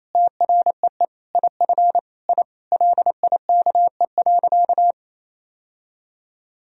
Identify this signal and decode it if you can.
Morse code, heard as a keyed tone, about 28 wpm, 700 Hz. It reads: TREE IF I LIKE.
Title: Sentences from Top 1000 Words - Review Part 1 - 28wpm